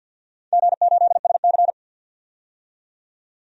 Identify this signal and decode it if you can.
Morse code, heard as a keyed tone, about 52 wpm, 700 Hz. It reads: G8SC
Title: DXCC-100 Call Signs - 4 Characters - 50wpm